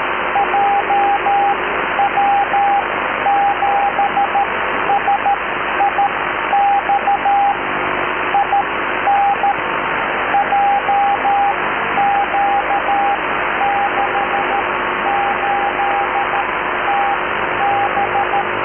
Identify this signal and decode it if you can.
Morse code, heard as a keyed tone, about 13 wpm, spaced as for 11 wpm, 800 Hz, with radio noise. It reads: JW7SIX IN JQ68TB